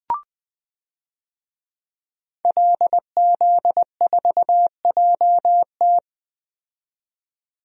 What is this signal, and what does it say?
Morse code, heard as a keyed tone, about 20 wpm, 700 Hz. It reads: LZ4JT
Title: DXCC-100 Call Signs - 5 Characters - 20wpm